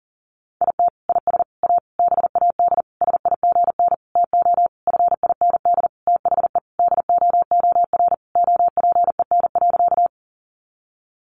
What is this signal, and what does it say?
Morse code, heard as a keyed tone, about 40 wpm, 700 Hz. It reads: IT IS A BAD SIGN TO FIND THE DOOR OPEN.
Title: Sentences from Top 1000 Words 40wpm